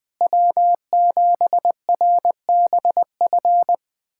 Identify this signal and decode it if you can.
Morse code, heard as a keyed tone, about 21 wpm, 700 Hz. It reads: W7RBF
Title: USA Callsign DE Callsign - 5 Characters - 20wpm